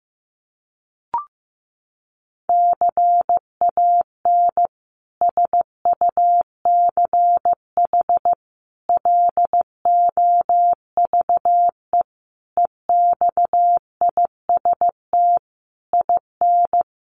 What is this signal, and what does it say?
Morse code, heard as a keyed tone, about 15 wpm, 700 Hz. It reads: CAN SUCH LOVE EXIST IN
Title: Sentences from Top 1100 Words 15wpm